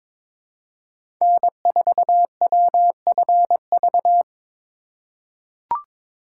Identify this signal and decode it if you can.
Morse code, heard as a keyed tone, about 22 wpm, 700 Hz. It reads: N4WFV
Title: USA Call Signs - 5 Characters - 4x Character Spacing - 22wpm